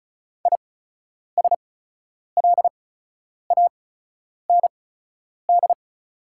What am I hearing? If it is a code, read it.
Morse code, heard as a keyed tone, about 34 wpm, spaced as for 10 wpm, 690 Hz. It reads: ISLAND
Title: 8x Character Spacing - Top 500 Words - 35wpm